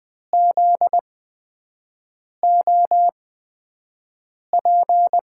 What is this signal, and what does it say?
Morse code, heard as a keyed tone, about 20 wpm, spaced as for 5 wpm, 700 Hz. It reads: ZOP
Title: USA Call Signs - 5 Characters - 8x Character Spacing - 20wpm